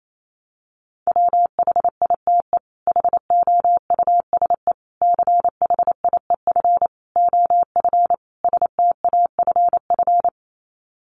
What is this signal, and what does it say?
Morse code, heard as a keyed tone, about 28 wpm, 700 Hz. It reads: WHITE HOUSE CHIEF OF STAFF
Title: Sets of 5 Words Encore 28wpm